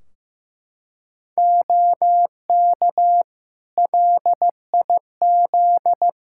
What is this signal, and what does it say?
Morse code, heard as a keyed tone, about 15 wpm, 700 Hz. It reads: OK LIZ
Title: K1USN SST QSO Practice 15wpm